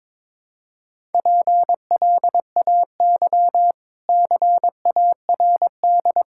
Morse code, heard as a keyed tone, about 22 wpm, 690 Hz. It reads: PLAY CARD
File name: practice / 22wpm / Easy Sentences with Hints - 22wpm